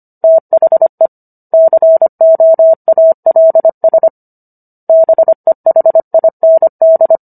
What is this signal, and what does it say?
Morse code, heard as a keyed tone, about 25 wpm, 650 Hz. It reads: THE COALS BEHIND